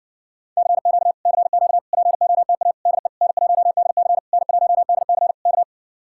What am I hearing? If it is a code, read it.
Morse code, heard as a keyed tone, about 60 wpm, 700 Hz. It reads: CQ CQ POTA DE N1DC N1DC K